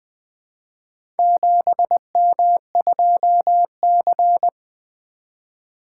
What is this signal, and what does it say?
Morse code, heard as a keyed tone, about 20 wpm, 700 Hz. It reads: 7M2C
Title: Four Letter-Number Combinations - 20wpm